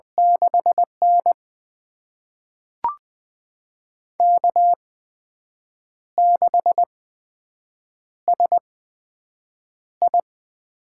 Morse code, heard as a keyed tone, about 20 wpm, spaced as for 7 wpm, 700 Hz. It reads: 6N K6SI
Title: USA Call Signs - 8x Character Spacing - 20wpm